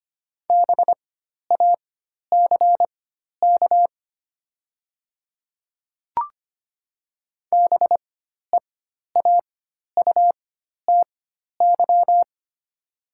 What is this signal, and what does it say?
Morse code, heard as a keyed tone, about 25 wpm, spaced as for 12 wpm, 700 Hz. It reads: BACK BEAUTY
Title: 4x Character Spacing - Top 500 Words - 25wpm